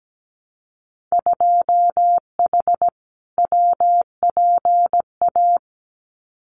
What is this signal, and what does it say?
Morse code, heard as a keyed tone, about 17 wpm, 700 Hz. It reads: 2H WPA
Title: Winter Field Day Exchanges 17wpm